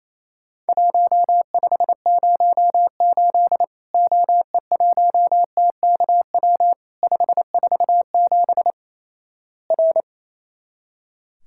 Morse code, heard as a keyed tone, about 28 wpm, 700 Hz. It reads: 1508 OE1TKW 547 R
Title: Worked All Europe DX Contest - QTC Exchange - 28wpm